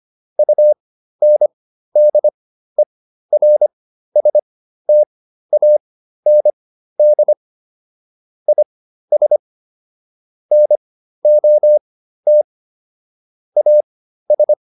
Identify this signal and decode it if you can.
Morse code, heard as a keyed tone, about 25 wpm, spaced as for 13 wpm, 600 Hz. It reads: UNDERSTAND IS NOT AS